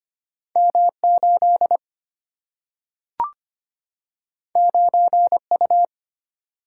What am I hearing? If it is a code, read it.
Morse code, heard as a keyed tone, about 25 wpm, 700 Hz. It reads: M8 9U
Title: Single-Letter Single-Number 25wpm